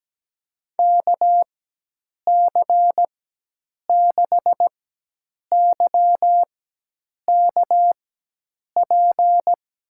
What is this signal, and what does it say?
Morse code, heard as a keyed tone, about 17 wpm, spaced as for 8 wpm, 700 Hz. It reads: KC6YKP
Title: USA Call Signs - 6 Characters - 4x Character Spacing - 17wpm